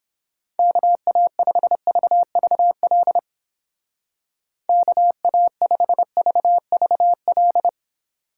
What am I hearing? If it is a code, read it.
Morse code, heard as a keyed tone, about 29 wpm, 690 Hz. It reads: KA5VVL KA5VVL